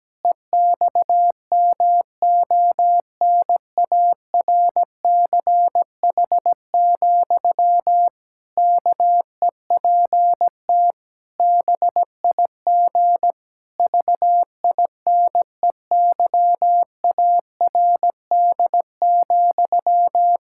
Morse code, heard as a keyed tone, about 17 wpm, 700 Hz. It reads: EXMONARCH, KEPT BIG VINEYARD,
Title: Pangram Challenge v2 - 17wpm